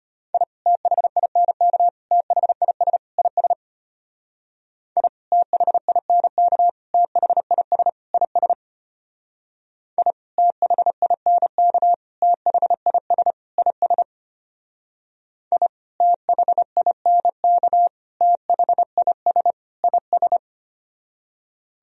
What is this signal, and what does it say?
Morse code, heard as a keyed tone, about 30 wpm, 700 Hz. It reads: I THINK THIS IS I THINK THIS IS I THINK THIS IS I THINK THIS IS
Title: Speed-Racing - Sets of 4 Words 25wpm